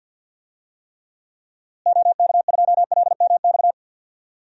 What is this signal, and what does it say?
Morse code, heard as a keyed tone, about 50 wpm, 700 Hz. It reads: OK2LMX
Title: DXCC-100 Call Signs - 6 Characters - Repeated 3x - 50wpm